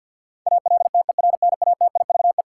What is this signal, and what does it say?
Morse code, heard as a keyed tone, about 53 wpm, 700 Hz. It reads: ALTERNATIVE
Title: Sentences from Top 1000 Words - Review Part 4 - New Word Drill - 50wpm